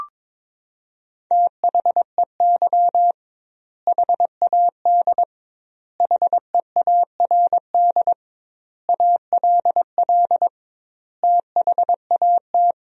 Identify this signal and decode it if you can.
Morse code, heard as a keyed tone, about 22 wpm, 700 Hz. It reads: THEY HAD HEARD ALL THAT
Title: 2x Word-Spacing - Sentences from Top 400 Words - 22wpm